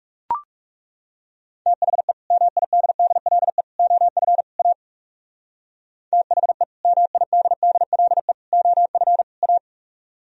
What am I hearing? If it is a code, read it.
Morse code, heard as a keyed tone, about 39 wpm, 700 Hz. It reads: THE MIDDLE OF A THE MIDDLE OF A